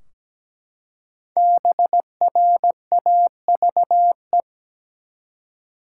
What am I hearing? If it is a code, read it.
Morse code, heard as a keyed tone, about 17 wpm, 700 Hz. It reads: BRAVE
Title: Five Letter Words - 17wpm